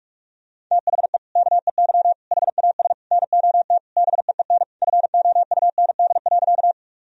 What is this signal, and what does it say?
Morse code, heard as a keyed tone, about 46 wpm, 700 Hz. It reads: THE KEY HAS NOT BEEN FOUND.
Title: Sentences from Top 1000 Words - Review Part 4 - 45wpm